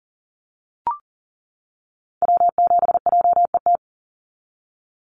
Morse code, heard as a keyed tone, about 40 wpm, 700 Hz. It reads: W7JET
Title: NA SOTA Activator Callsigns 40wpm